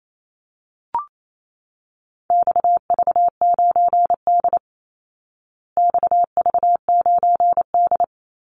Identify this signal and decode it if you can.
Morse code, heard as a keyed tone, about 28 wpm, 700 Hz. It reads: XV9D XV9D
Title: Call Signs Repeated 3x - 28wpm